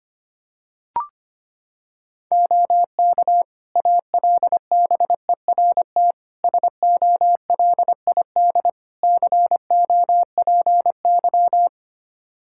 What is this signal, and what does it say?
Morse code, heard as a keyed tone, about 25 wpm, 700 Hz. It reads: OK ALBERT SOLID COPY
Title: Practice QSO - Exchange 3 - More Info - 25wpm